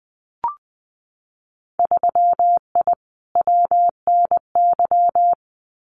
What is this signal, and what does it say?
Morse code, heard as a keyed tone, about 20 wpm, 700 Hz. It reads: 3I WNY
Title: Winter Field Day Exchanges Encore 20wpm